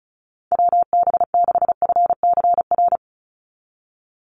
Morse code, heard as a keyed tone, about 34 wpm, 700 Hz. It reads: WB6FCR
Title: USA Call Signs 35wpm